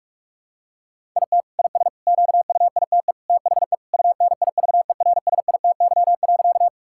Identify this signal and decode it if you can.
Morse code, heard as a keyed tone, about 46 wpm, 700 Hz. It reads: IT IS QUITE THE UNIVERSITY.
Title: Sentences from Top 1000 Words - Review Part 5 - 45wpm